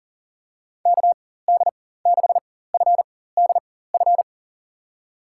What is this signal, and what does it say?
Morse code, heard as a keyed tone, about 37 wpm, spaced as for 18 wpm, 700 Hz. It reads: KD6FDF
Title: USA Call Signs - 6 Characters - 4x Character Spacing - 40wpm